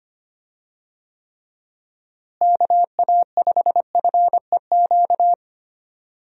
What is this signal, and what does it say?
Morse code, heard as a keyed tone, about 25 wpm, 700 Hz. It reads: KA5FEQ
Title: USA Call Signs - 6 Characters - Repeated 3x - 25wpm